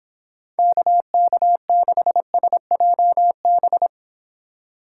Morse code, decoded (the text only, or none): KK6SJB